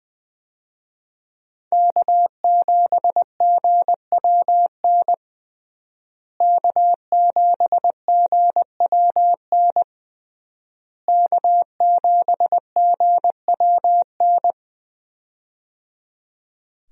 Morse code, heard as a keyed tone, about 20 wpm, 700 Hz. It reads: K7GWN K7GWN K7GWN
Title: USA Call Signs - 5 Characters - Repeated 3x - 20wpm